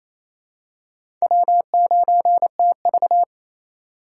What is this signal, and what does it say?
Morse code, heard as a keyed tone, about 28 wpm, 700 Hz. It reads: W9TV
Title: USA Call Signs - 4 Characters - 3x Character Spacing - 28wpm